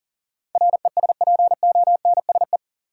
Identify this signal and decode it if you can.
Morse code, heard as a keyed tone, about 39 wpm, 700 Hz. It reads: RESPONSE
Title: Sentences from Top 1400 Words - New Word Drill - 40wpm